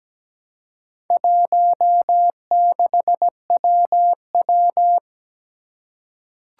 Morse code, heard as a keyed tone, about 17 wpm, 700 Hz. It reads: 16WW
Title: Four Challenging Letter-Number Combinations 17wpm